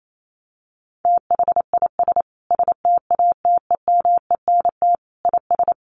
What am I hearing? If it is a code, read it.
Morse code, heard as a keyed tone, about 28 wpm, 700 Hz. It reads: THIS STATEMENT IS
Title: Sentences from Top 1300 Words 28wpm